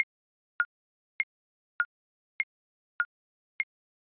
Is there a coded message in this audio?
No.